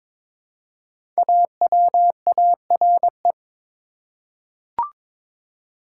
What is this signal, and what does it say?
Morse code, heard as a keyed tone, about 22 wpm, 700 Hz. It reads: AWARE
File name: 5 Letter Words - First 3 Letters Given - 22wpm